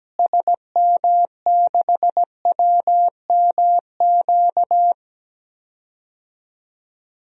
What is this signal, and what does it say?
Morse code, heard as a keyed tone, about 17 wpm, 700 Hz. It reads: SM6WMQ
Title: DXCC-100 Call Signs - 6 Characters - Repeated 3x - 17wpm